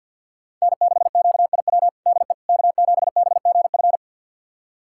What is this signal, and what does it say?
Morse code, heard as a keyed tone, about 48 wpm, 690 Hz. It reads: N6QIW DE K7BGF
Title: USA Callsign DE Callsign - 5 Characters - 50wpm